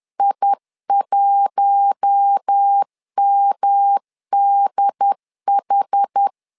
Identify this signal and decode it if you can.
Morse code, heard as a keyed tone, about 10 wpm, 800 Hz. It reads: I1MDH
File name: LEZIONI / 4 Q1 10wpm 2448